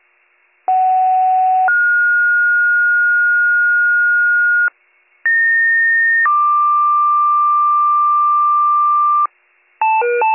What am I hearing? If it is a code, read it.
Two-tone page, heard as a one-second tone, then a three-second tone, about 750 then 1430 Hz, about 1850 then 1180 Hz, with noise. It is heard from 0.7 to 9.3 s.